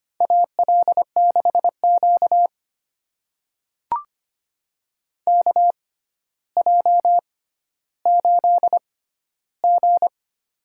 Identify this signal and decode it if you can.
Morse code, heard as a keyed tone, about 25 wpm, spaced as for 13 wpm, 700 Hz. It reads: AL6Q KJ8G
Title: USA Call Signs - 4 Characters - 6x Character Spacing - 25wpm